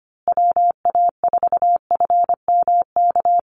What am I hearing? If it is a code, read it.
Morse code, heard as a keyed tone, about 25 wpm, 700 Hz. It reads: WA4FMK
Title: USA Call Signs Repeated 3x 25wpm